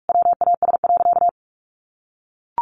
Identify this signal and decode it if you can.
Morse code, heard as a keyed tone, about 44 wpm, 700 Hz. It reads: WAS.
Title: Sentences from Top 1400 Words 45wpm